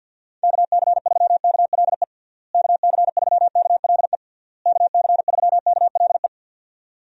Morse code, heard as a keyed tone, about 52 wpm, 700 Hz. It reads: KC3KLE KC3KLE KC3KLE
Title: USA Call Signs - 6 Characters - Repeated 3x - 50wpm